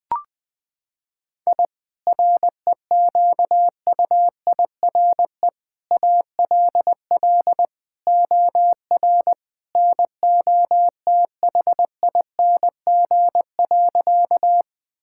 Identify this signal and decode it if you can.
Morse code, heard as a keyed tone, about 21 wpm, 700 Hz. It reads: I REQUIRE ALL OR NOTHING.
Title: Sentences from Top 1000 Words - Review Part 6 - 20wpm